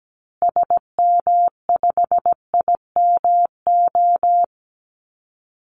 Morse code, heard as a keyed tone, about 17 wpm, 700 Hz. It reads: SM5IMO
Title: Call Signs Repeated 3x - 17wpm